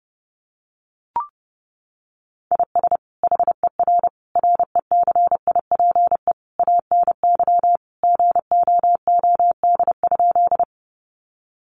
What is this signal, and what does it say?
Morse code, heard as a keyed tone, about 30 wpm, 700 Hz. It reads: IS HER RECIPE ANY GOOD?
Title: Sentences from Top 1200 Words 30wpm